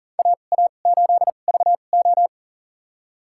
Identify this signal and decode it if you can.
Morse code, heard as a keyed tone, about 40 wpm, spaced as for 29 wpm, 700 Hz. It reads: AA8VO
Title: USA Call Signs - 5 Characters - 2x Character Spacing - 40wpm